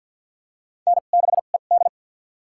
Morse code, heard as a keyed tone, about 48 wpm, spaced as for 35 wpm, 700 Hz. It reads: N6ED
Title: USA Call Signs - 4 Characters - 2x Character Spacing - 50wpm